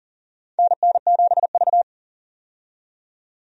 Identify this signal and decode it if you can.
Morse code, heard as a keyed tone, about 43 wpm, 690 Hz. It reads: NN7V
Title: USA Call Signs - 4 Characters - 40wpm